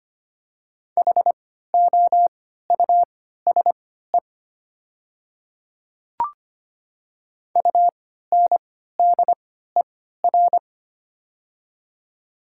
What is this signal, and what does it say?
Morse code, heard as a keyed tone, about 25 wpm, spaced as for 14 wpm, 690 Hz. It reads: HOUSE UNDER
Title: 3x Character Spacing - Top 200 Words - 25wpm